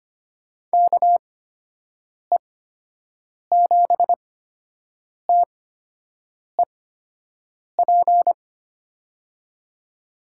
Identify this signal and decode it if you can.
Morse code, heard as a keyed tone, about 25 wpm, spaced as for 7 wpm, 700 Hz. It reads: KE7TEP
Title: USA Call Signs - 8x Character Spacing - 25wpm